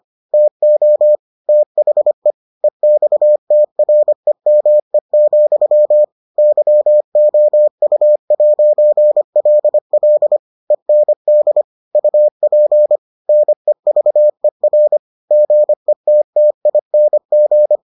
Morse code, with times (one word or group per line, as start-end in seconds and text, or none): 0.3-1.2 TO
1.5-2.3 THE
2.6-6.0 EXTREME,
6.4-10.4 YOU'LL
10.7-11.6 END
11.9-13.0 UP
13.3-15.0 NEVER
15.3-17.8 GETTING